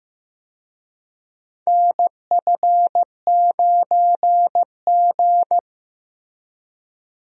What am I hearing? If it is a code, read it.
Morse code, heard as a keyed tone, about 14 wpm, 700 Hz. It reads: NF9G
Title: USA Call Signs - 4 Characters - Repeated 3x - 15wpm